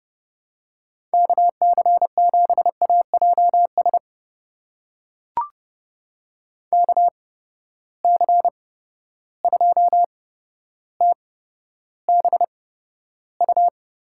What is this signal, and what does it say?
Morse code, heard as a keyed tone, about 30 wpm, spaced as for 13 wpm, 700 Hz. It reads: KC7AJS KC2TBU
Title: USA Call Signs - 6 Characters - 8x Character Spacing - 30wpm